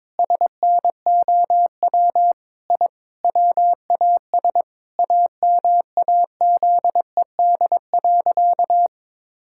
Morse code, heard as a keyed tone, about 23 wpm, 700 Hz. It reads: SNOW I WAS AMAZED.
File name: practice / 22wpm / Sentences with 4-Word Phrases - 22wpm